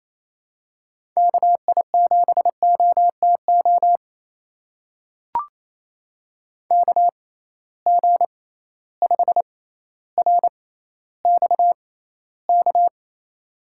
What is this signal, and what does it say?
Morse code, heard as a keyed tone, about 28 wpm, spaced as for 14 wpm, 700 Hz. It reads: KI7OTO KG5RXK
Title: USA Call Signs - 6 Characters - 6x Character Spacing - 28wpm